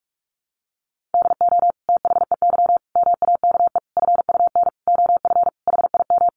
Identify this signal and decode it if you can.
Morse code, heard as a keyed tone, about 44 wpm, 700 Hz. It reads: DO THEY MAKE FUN OF HIM
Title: Sentences from Top 1300 Words 45wpm